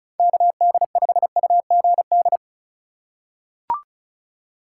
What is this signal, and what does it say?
Morse code, heard as a keyed tone, about 36 wpm, 700 Hz. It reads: KD5UGD
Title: USA Call Signs - 6 Characters - 2x Character Spacing - 35wpm